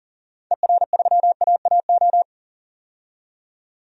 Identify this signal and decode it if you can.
Morse code, heard as a keyed tone, about 40 wpm, 700 Hz. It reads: ER3AAO
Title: DXCC-100 Call Signs - 6 Characters - 40wpm